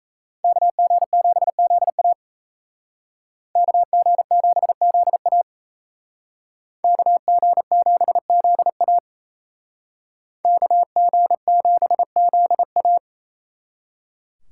Morse code, heard as a keyed tone, about 35 wpm, 700 Hz. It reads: KG7ZA KG7ZA KG7ZA KG7ZA